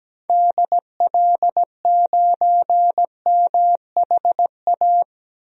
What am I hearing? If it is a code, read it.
Morse code, heard as a keyed tone, about 17 wpm, 700 Hz. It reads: DL9MHA
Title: DXCC-100 Call Signs - 6 Characters - 17wpm